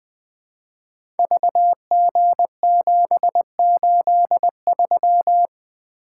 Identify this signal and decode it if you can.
Morse code, heard as a keyed tone, about 20 wpm, 700 Hz. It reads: VG783
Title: Code Groups - 5 Letter:Numbers - 20wpm